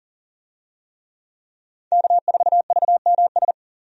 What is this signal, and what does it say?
Morse code, heard as a keyed tone, about 38 wpm, 700 Hz. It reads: K4VMS
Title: USA Call Signs - 5 Characters - 40wpm